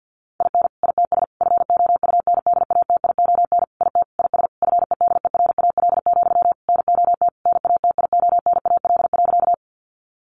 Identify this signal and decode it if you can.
Morse code, heard as a keyed tone, about 50 wpm, 700 Hz. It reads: IN ITS FOUNDATION IT IS FEDERAL, NOT NATIONAL.
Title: Sentences from Top 1200 Words 50wpm